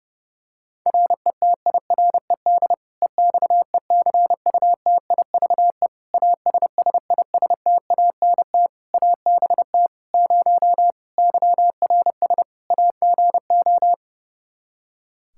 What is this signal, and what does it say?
Morse code, heard as a keyed tone, about 30 wpm, 700 Hz. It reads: RETIRED EXECUTIVE ASSISTANT ABT 0 YRS AGO
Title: K1USN SST QSO Practice 30wpm